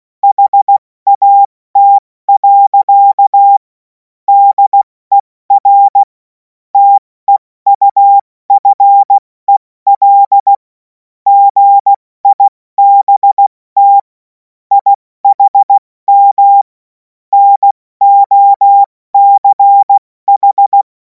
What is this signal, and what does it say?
Morse code, heard as a keyed tone, about 16 wpm, 800 Hz. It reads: HAT. DER TEUFEL GIBT IHM NOCH
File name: Morsekurs / Witze / Witz 03. März 2026 70 F80